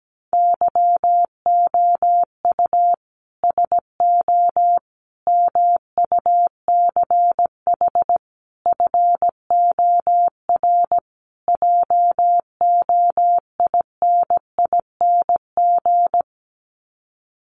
Morse code, heard as a keyed tone, about 17 wpm, 700 Hz. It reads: YOU SO MUCH FOR JOINING
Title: Sets of 5 Words Encore2 17wpm